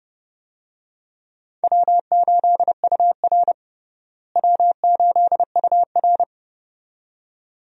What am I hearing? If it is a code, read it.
Morse code, heard as a keyed tone, about 30 wpm, 700 Hz. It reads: W8UR W8UR
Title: USA Call Signs - 4 Characters - Repeated 2x - 30wpm